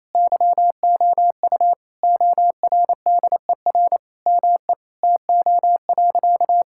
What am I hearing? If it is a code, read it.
Morse code, heard as a keyed tone, about 28 wpm, 700 Hz. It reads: YOU ORDER ME TO.
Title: Sentences from Top 1000 Words - Review Part 6 - 28wpm